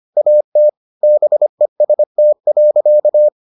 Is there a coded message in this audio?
Yes: Morse code from 0.2 to 3.3 s, heard as a keyed tone, about 25 wpm, 600 Hz. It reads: AT BEST.